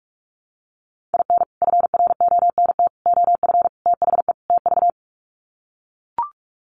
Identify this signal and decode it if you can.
Morse code, heard as a keyed tone, about 44 wpm, 700 Hz. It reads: IN FRONT OF THE TV
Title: Sets of 5 Words Encore2 45wpm